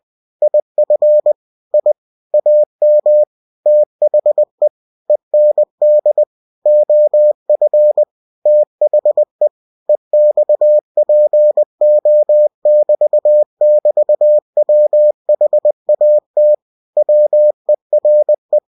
Morse code, heard as a keyed tone, about 20 wpm, 600 Hz. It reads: IF I AM THE END OF THE EXPO<BT><BT>WHAT WERE
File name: cw-books / winnie-the-pooh / winnie-20wpm-page-0079